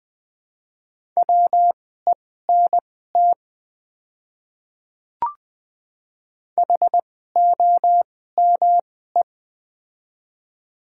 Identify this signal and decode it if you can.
Morse code, heard as a keyed tone, about 20 wpm, spaced as for 14 wpm, 700 Hz. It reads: WENT HOME
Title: 2x Character Spacing - Top 300 Words - 20wpm